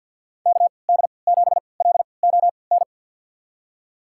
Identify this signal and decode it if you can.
Morse code, heard as a keyed tone, about 51 wpm, spaced as for 28 wpm, 700 Hz. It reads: KD7LON
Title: USA Call Signs - 6 Characters - 3x Character Spacing - 50wpm